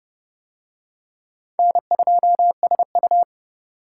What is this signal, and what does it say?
Morse code, heard as a keyed tone, about 29 wpm, 700 Hz. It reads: N2SU